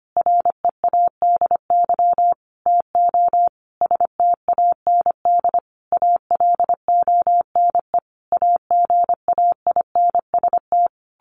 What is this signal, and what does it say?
Morse code, heard as a keyed tone, about 25 wpm, 700 Hz. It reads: READY TO STAND ALONE AGAINST
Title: Sentences from Top 1400 Words 25wpm